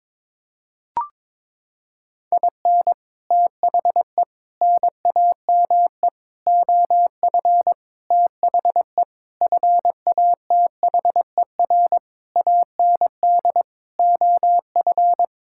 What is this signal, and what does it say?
Morse code, heard as a keyed tone, about 22 wpm, 700 Hz. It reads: IN THE NAME OF THE FATHER AND OF
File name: Sets of 9 Words - 22wpm